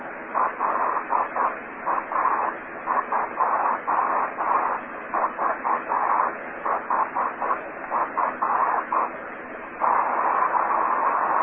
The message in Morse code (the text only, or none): none